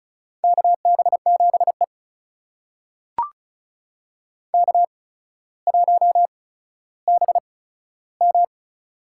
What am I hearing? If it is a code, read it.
Morse code, heard as a keyed tone, about 36 wpm, spaced as for 15 wpm, 700 Hz. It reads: KB7E K1BM
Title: USA Call Signs - 8x Character Spacing - 35wpm